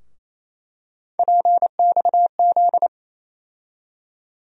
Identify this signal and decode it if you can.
Morse code, heard as a keyed tone, about 28 wpm, 700 Hz. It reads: PXZ
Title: USA Call Sign Postfixes - 3 Letters 28wpm